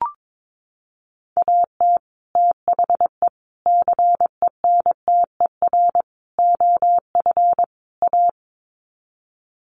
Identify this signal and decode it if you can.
Morse code, heard as a keyed tone, about 22 wpm, 700 Hz. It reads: AT THE CENTER OF A
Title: Sets of 5 Words Encore2 22wpm